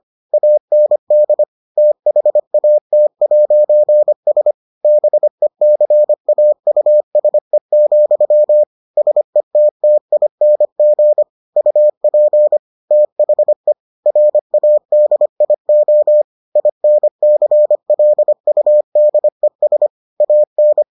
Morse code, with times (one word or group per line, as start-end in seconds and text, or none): 0.3-1.4 AND
1.8-4.5 THAT'S
4.8-8.6 BECAUSE,
9.0-11.2 SETTING
11.6-12.6 UP
12.9-13.7 THE
14.1-16.2 RADIO
16.6-19.9 INCLUDES
20.2-20.8 AN